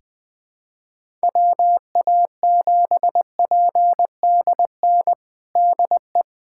Morse code, heard as a keyed tone, about 21 wpm, 700 Hz. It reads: WA7PDN DE
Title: USA Callsign DE Callsign 20wpm